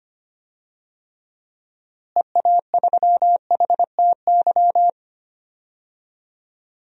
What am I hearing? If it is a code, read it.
Morse code, heard as a keyed tone, about 25 wpm, 700 Hz. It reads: EA3HTY
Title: DXCC-100 Call Signs - 6 Characters - Repeated 2x - 25wpm